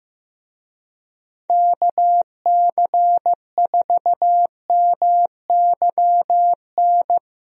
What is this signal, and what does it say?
Morse code, heard as a keyed tone, about 16 wpm, 700 Hz. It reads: KC4MYN
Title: USA Call Signs - 6 Characters - Repeated 2x - 15wpm